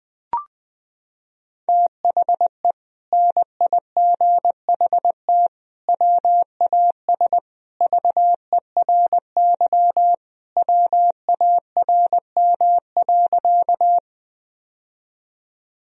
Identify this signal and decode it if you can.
Morse code, heard as a keyed tone, about 20 wpm, 700 Hz. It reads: THE NIGHT WAS VERY WARM.
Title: Sentences from Top 1000 Words - Review Part 2 - 20wpm